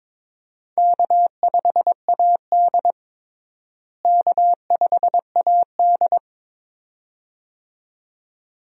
Morse code, heard as a keyed tone, about 21 wpm, 700 Hz. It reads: K5AD K5AD
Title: USA Call Signs - 4 Characters - Repeated 2x - 22wpm